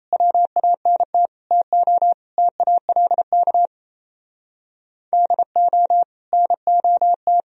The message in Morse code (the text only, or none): WANT TO TALK DO NOT